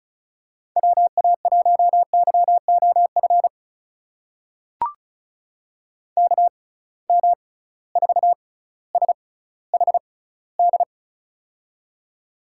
Morse code, heard as a keyed tone, about 36 wpm, spaced as for 18 wpm, 700 Hz. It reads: WA1YOF KM4SHD
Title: USA Call Signs - 6 Characters - 6x Character Spacing - 35wpm